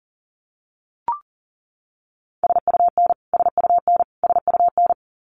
Morse code, heard as a keyed tone, about 37 wpm, 700 Hz. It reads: SUN SUN SUN
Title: Top 500 Words Repeated 40wpm